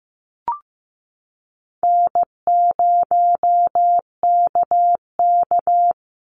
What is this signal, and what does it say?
Morse code, heard as a keyed tone, about 15 wpm, 700 Hz. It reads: N0KK
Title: Easy Call Signs Repeated 3x - 15wpm